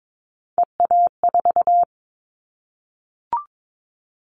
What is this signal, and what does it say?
Morse code, heard as a keyed tone, about 22 wpm, 700 Hz. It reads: EA4
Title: Callsign Prefixes for Top 25 DXCC - Repeated 3x - 22wpm